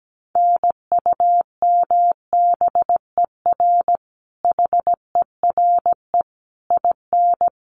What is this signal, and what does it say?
Morse code, heard as a keyed tone, about 17 wpm, 700 Hz. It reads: NUMBER HERE IN
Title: Sets of 5 Words Encore2 17wpm